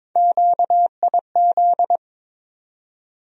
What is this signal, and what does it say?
Morse code, heard as a keyed tone, about 22 wpm, 700 Hz. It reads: QIZ
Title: USA Call Sign Postfixes - 3 Letters 22wpm